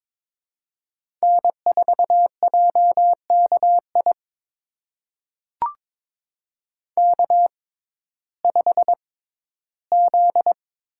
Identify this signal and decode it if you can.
Morse code, heard as a keyed tone, about 22 wpm, spaced as for 14 wpm, 700 Hz. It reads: N4JKI K5Z
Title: USA Call Signs - 5 Characters - 6x Character Spacing - 22wpm